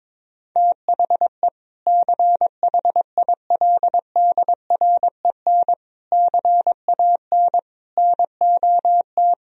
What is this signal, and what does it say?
Morse code, heard as a keyed tone, about 22 wpm, 700 Hz. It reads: THE CHILDREN CAN NOT
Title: Sentences from Top 1000 Words - Review Part 4 - 22wpm